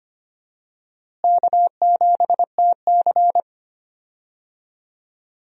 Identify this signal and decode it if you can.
Morse code, heard as a keyed tone, about 25 wpm, 700 Hz. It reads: K7TC
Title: USA Call Signs - 4 Characters - Repeated 2x - 25wpm